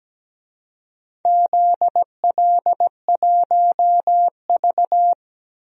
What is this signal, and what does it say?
Morse code, heard as a keyed tone, about 17 wpm, 700 Hz. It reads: ZL1V
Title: DXCC-100 Call Signs - 4 Characters - 17wpm